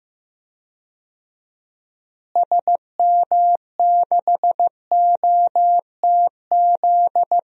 Morse code, heard as a keyed tone, about 15 wpm, 700 Hz. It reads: SM6OTZ
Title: DXCC-100 Call Signs - 6 Characters - Repeated 2x - 15wpm